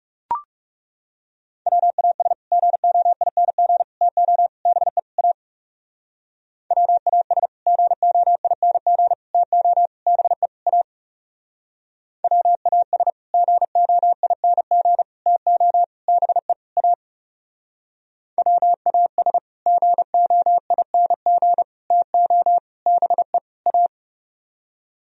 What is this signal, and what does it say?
Morse code, heard as a keyed tone, about 36 wpm, 700 Hz. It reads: WAS GOING TO BE A WAS GOING TO BE A WAS GOING TO BE A WAS GOING TO BE A
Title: Speed-Racing - Sets of 5 Words 30wpm